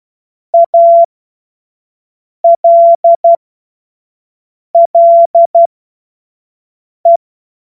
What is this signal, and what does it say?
Morse code, heard as a keyed tone, about 12 wpm, spaced as for 5 wpm, 680 Hz. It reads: ALLE
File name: Morsekurs / Witze / Witz 07. Juni 2022 25 F70